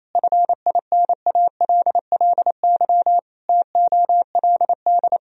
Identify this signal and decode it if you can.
Morse code, heard as a keyed tone, about 28 wpm, 700 Hz. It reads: FINALLY TOLD
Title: Sentences from Top 1000 Words - Review Part 3 - 28wpm